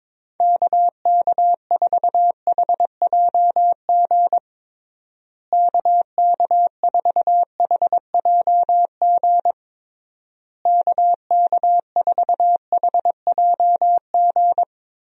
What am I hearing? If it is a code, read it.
Morse code, heard as a keyed tone, about 22 wpm, 700 Hz. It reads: KK4HJG KK4HJG KK4HJG
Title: USA Call Signs - 6 Characters - Repeated 3x - 22wpm